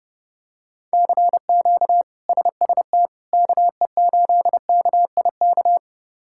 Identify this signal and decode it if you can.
Morse code, heard as a keyed tone, about 30 wpm, 700 Hz. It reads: CQ SST KE8KIK
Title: K1USN SST QSO Practice - 30wpm